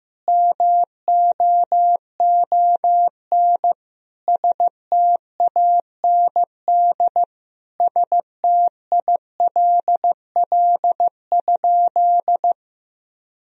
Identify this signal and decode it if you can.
Morse code, heard as a keyed tone, about 15 wpm, 700 Hz. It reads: MOON STAND STILL?
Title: Sentences from Top 1000 Words - Review Part 3 - 15wpm